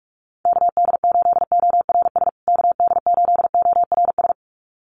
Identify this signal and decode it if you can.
Morse code, heard as a keyed tone, about 44 wpm, 700 Hz. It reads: KD8ORS KD8ORS
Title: USA Call Signs Repeated 2x 45wpm